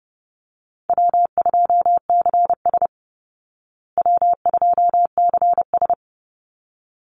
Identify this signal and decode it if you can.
Morse code, heard as a keyed tone, about 30 wpm, 700 Hz. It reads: W2CS W2CS
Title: Call Signs Repeated 3x - 30wpm